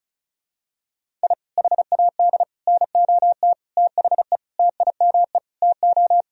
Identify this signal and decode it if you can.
Morse code, heard as a keyed tone, about 34 wpm, 700 Hz. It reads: I HAD NOT THE TIME TO
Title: Sentences from Top 1000 Words - Review Part 1 - 35wpm